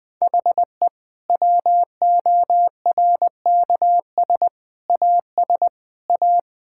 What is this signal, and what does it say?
Morse code, heard as a keyed tone, about 20 wpm, 700 Hz. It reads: HE WORKS AS A
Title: Difficult Sentences with Hints - 20wpm